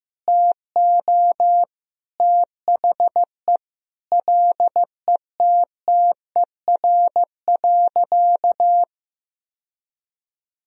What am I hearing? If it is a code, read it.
Morse code, heard as a keyed tone, about 15 wpm, 700 Hz. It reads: TO THE LETTER.
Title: Sentences from Top 1000 Words - Review Part 1 - 15wpm